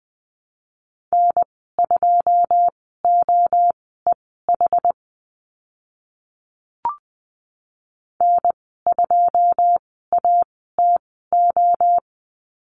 Morse code, heard as a keyed tone, about 20 wpm, spaced as for 14 wpm, 700 Hz. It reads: N2OEH N2ATO
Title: 161 Lesson 26g - Letter g - Call Signs 20wpm